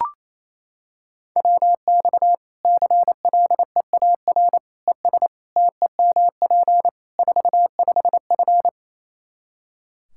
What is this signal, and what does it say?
Morse code, heard as a keyed tone, about 28 wpm, 700 Hz. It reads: WX CLEAR ES TEMP 45F
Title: Practice QSO - Exchange 2 - Info - 28wpm